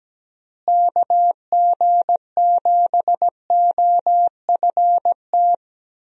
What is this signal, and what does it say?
Morse code, heard as a keyed tone, about 17 wpm, 700 Hz. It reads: KG7OFT
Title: USA Call Signs - 6 Characters - Repeated 3x - 17wpm